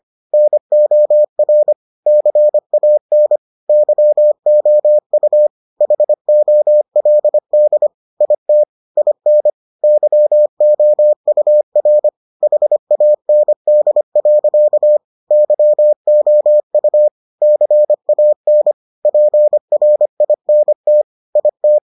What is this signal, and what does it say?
Morse code, heard as a keyed tone, about 25 wpm, 600 Hz. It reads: NOR CAN YOU HOLD IT IN YOUR HAND. YOU CAN PRINT IT